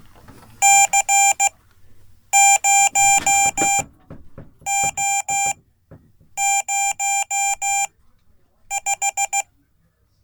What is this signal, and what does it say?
Morse code, heard as a keyed tone, about 16 wpm, spaced as for 8 wpm, 790 Hz. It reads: C0O05
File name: Code / 5min-b